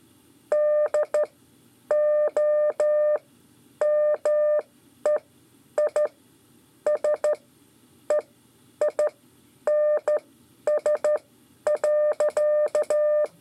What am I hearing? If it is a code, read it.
Morse code, heard as a keyed tone, about 13 wpm, spaced as for 9 wpm, 600 Hz. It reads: DOMEISEINS.
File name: 5 LEZIONE RIPASSO 1